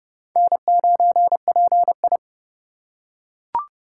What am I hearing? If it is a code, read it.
Morse code, heard as a keyed tone, about 29 wpm, 700 Hz. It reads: N9PI
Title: USA Call Signs - 4 Characters - 3x Character Spacing - 30wpm